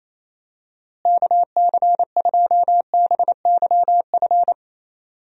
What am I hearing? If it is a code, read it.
Morse code, heard as a keyed tone, about 28 wpm, 700 Hz. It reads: KC2BYF
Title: USA Call Signs - 6 Characters - Repeated 3x - 28wpm